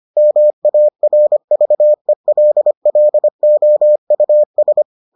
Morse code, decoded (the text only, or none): MARVELLOUS